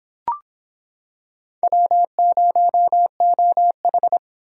Morse code, heard as a keyed tone, about 26 wpm, 700 Hz. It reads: W0OH